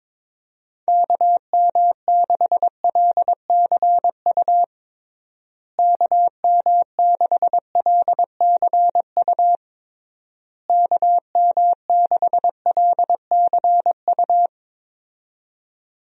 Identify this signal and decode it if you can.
Morse code, heard as a keyed tone, about 21 wpm, 700 Hz. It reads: KM6LCU KM6LCU KM6LCU
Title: USA Call Signs - 6 Characters - Repeated 3x - 22wpm